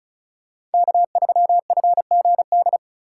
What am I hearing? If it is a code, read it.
Morse code, heard as a keyed tone, about 34 wpm, 700 Hz. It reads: K3FGD
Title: USA Call Signs - 2x Character Spacing - 35wpm